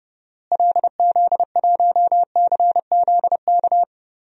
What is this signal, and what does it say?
Morse code, heard as a keyed tone, about 30 wpm, 700 Hz. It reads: LZ1CZK
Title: DXCC-100 Call Signs - 6 Characters - Repeated 3x - 30wpm